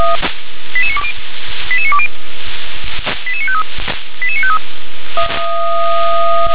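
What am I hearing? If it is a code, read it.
Five-tone selective call (ZVEI), heard as quick runs of five tones, tones of about 70 ms. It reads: 88920 88920 88953 88953
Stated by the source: Five-tone ZVEI selective call received on SDR.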